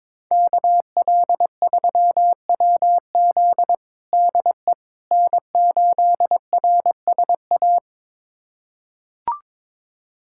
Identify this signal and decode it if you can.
Morse code, heard as a keyed tone, about 21 wpm, 700 Hz. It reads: KL3WZ DE N8RSA
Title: USA Callsign DE Callsign - 5 Characters - 22wpm